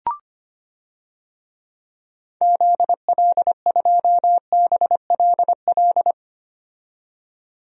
Morse code, heard as a keyed tone, about 25 wpm, 700 Hz. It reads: ZL2BLL
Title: DXCC-100 Call Signs - 6 Characters - 25wpm